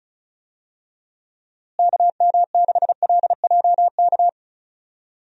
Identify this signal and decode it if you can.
Morse code, heard as a keyed tone, about 34 wpm, 700 Hz. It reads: KM6LJK
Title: USA Call Signs - 6 Characters - Repeated 2x - 35wpm